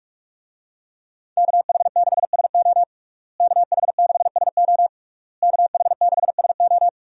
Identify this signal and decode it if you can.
Morse code, heard as a keyed tone, about 46 wpm, 700 Hz. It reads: KH6SO KH6SO KH6SO
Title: USA Call Signs - 5 Characters - Repeated 3x - 45wpm